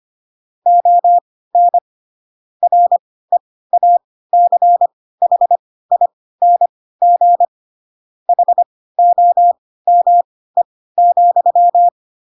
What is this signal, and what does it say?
Morse code, heard as a keyed tone, about 25 wpm, spaced as for 16 wpm, 700 Hz. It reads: ON REACHING HOME,
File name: Pinocchio / Pinocchio ch4 10 25 ik1hge